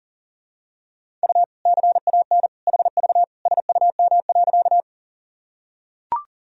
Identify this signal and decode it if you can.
Morse code, heard as a keyed tone, about 37 wpm, 700 Hz. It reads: U CAN HV SUM.
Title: Sentences from Top 100 Words and Abbreviations 40wpm